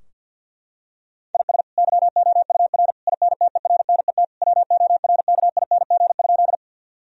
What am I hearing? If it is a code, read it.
Morse code, heard as a keyed tone, about 50 wpm, 700 Hz. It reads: IS YOUR INTERNET WORKING?
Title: Sentences from Top 1000 Words - Review Part 4 - 50wpm